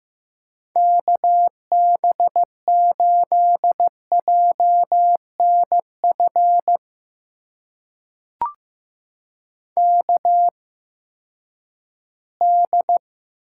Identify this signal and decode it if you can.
Morse code, heard as a keyed tone, about 15 wpm, spaced as for 10 wpm, 700 Hz. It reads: KB8JNF KD
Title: USA Call Signs - 6 Characters - 8x Character Spacing - 15wpm